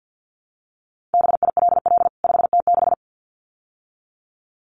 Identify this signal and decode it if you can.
Morse code, heard as a keyed tone, about 50 wpm, 700 Hz. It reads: BILL 5T6
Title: CWT Exchange 50wpm